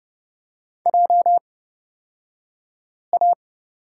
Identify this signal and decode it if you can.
Morse code, heard as a keyed tone, about 30 wpm, spaced as for 5 wpm, 700 Hz. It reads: JA
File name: Lesson 37d - Letter x - Two Characters 30wpm